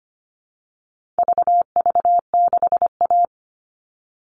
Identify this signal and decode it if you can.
Morse code, heard as a keyed tone, about 25 wpm, 700 Hz. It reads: VV6A
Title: All Cut Numbers - 4-Digit 25wpm